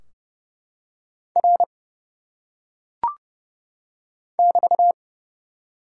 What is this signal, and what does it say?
Morse code, heard as a keyed tone, about 28 wpm, 700 Hz. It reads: R <BT>
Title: Top 90 QSO Elements 30wpm